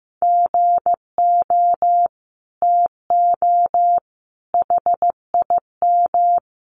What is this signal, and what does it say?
Morse code, heard as a keyed tone, about 15 wpm, 700 Hz. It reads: GO TO HIM
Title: Sentences from Top 1100 Words 15wpm